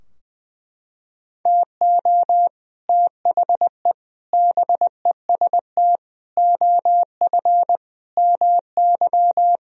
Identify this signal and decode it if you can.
Morse code, heard as a keyed tone, about 21 wpm, 700 Hz. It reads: TO THE BEST OF MY
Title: Sentences from Top 1000 Words - Review Part 3 - 20wpm